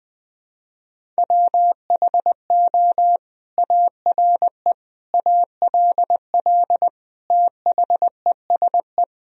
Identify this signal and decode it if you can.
Morse code, heard as a keyed tone, about 20 wpm, 700 Hz. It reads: WHO ARE ALL THESE
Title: Sentences from Top 1000 Words - Review Part 1 - 20wpm